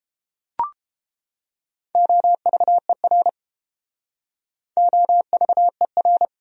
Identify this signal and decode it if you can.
Morse code, heard as a keyed tone, about 29 wpm, 700 Hz. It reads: OVER OVER